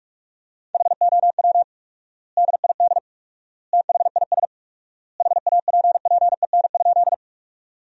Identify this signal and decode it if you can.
Morse code, heard as a keyed tone, about 46 wpm, 700 Hz. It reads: HOW DID THIS HAPPEN?
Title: 4x Word-Spacing - Sentences from Top 400 Words - 45wpm